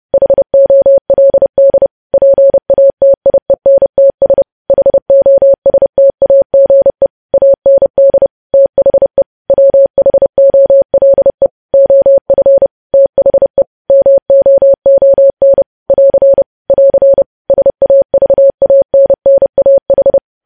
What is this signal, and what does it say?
Morse code, heard as a keyed tone, about 30 wpm, 550 Hz. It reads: HOLD PATIENTS HOSTAGE AND THE WHOLE OF THE MOON <AR> <AR> SAVANNAH